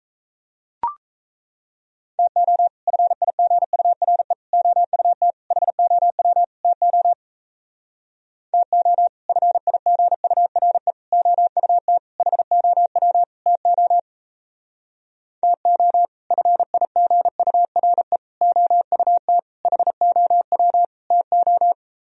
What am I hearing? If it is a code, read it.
Morse code, heard as a keyed tone, about 36 wpm, 700 Hz. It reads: TO FIGURE OUT HOW TO TO FIGURE OUT HOW TO TO FIGURE OUT HOW TO